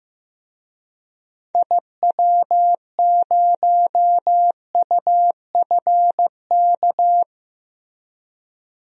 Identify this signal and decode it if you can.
Morse code, heard as a keyed tone, about 16 wpm, 700 Hz. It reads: IW0UFK
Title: DXCC-100 Call Signs - 6 Characters - Repeated 2x - 15wpm